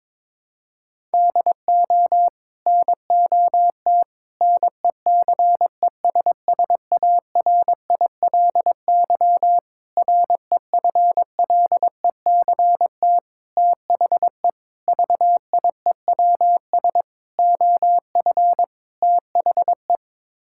Morse code, heard as a keyed tone, about 21 wpm, 700 Hz. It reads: DO NOT NECESSARILY REFLECT THE VIEWS OF THE
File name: practice / 22wpm / Sets of 8 Words - 22wpm